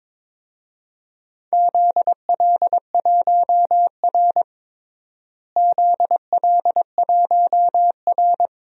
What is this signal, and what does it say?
Morse code, heard as a keyed tone, about 21 wpm, 700 Hz. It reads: ZL1R ZL1R
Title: DXCC-100 Call Signs - 4 Characters - Repeated 3x - 22wpm